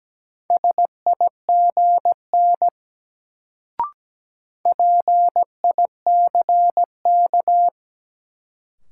Morse code, heard as a keyed tone, about 17 wpm, 700 Hz. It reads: SIGN PICK
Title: Sentences from Top 1000 Words - New Word Drill - 17wpm